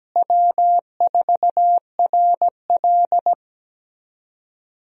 Morse code, heard as a keyed tone, about 17 wpm, 700 Hz. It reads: W4RL
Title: USA Call Signs - 4 Characters - Repeated 2x - 17wpm